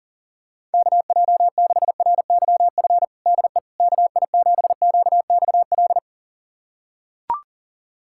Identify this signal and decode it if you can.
Morse code, heard as a keyed tone, about 37 wpm, 700 Hz. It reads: KJ6RYF DE KI7QXL
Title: USA Callsign DE Callsign - 6 Characters - 40wpm